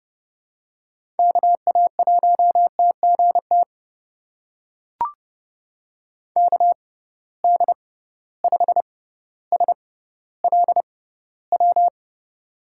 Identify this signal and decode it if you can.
Morse code, heard as a keyed tone, about 30 wpm, spaced as for 15 wpm, 700 Hz. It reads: KA1TGT KD5SLW
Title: USA Call Signs - 6 Characters - 6x Character Spacing - 30wpm